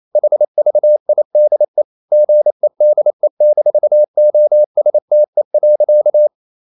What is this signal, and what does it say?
Morse code, heard as a keyed tone, about 28 wpm, 600 Hz. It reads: HVIDE GEDE-OSTE.
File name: Ferie QRQ 28wpm